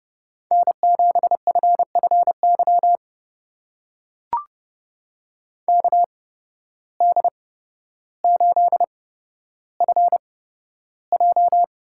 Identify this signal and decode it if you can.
Morse code, heard as a keyed tone, about 30 wpm, spaced as for 13 wpm, 700 Hz. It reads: N7FFY KD8FJ
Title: USA Call Signs - 8x Character Spacing - 30wpm